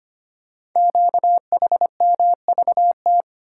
Morse code, heard as a keyed tone, about 25 wpm, 700 Hz. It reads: QHMVT
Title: Code Groups - 5 Letters - 25wpm